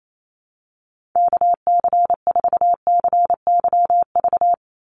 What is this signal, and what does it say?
Morse code, heard as a keyed tone, about 28 wpm, 700 Hz. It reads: KC4CYV
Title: USA Call Signs 28wpm